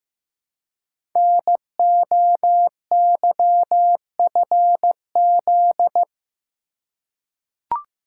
Morse code, heard as a keyed tone, about 15 wpm, 700 Hz. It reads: NOYFZ
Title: Code Groups - 5 Letters - 15wpm